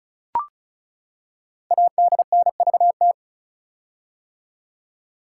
Morse code, heard as a keyed tone, about 36 wpm, 700 Hz. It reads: ADNVT
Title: Code Groups - 5 Letters - 35wpm